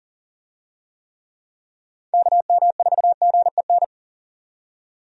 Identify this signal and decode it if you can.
Morse code, heard as a keyed tone, about 40 wpm, 700 Hz. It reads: KM4GEN
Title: USA Call Signs - 6 Characters - 40wpm